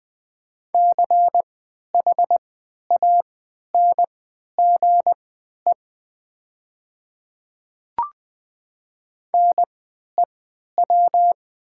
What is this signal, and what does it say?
Morse code, heard as a keyed tone, about 20 wpm, spaced as for 11 wpm, 700 Hz. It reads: CHANGE NEW
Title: 3x Character Spacing - Top 200 Words - 20wpm